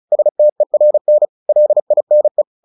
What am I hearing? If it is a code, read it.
Morse code, heard as a keyed tone, about 36 wpm, 600 Hz. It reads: STERN LINE
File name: the old man and the sea / the old man and the sea 1 35wpm